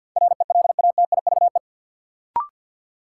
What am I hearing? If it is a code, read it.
Morse code, heard as a keyed tone, about 49 wpm, 700 Hz. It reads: RELATIVE
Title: Sentences from Top 1400 Words - New Word Drill Alt - 50wpm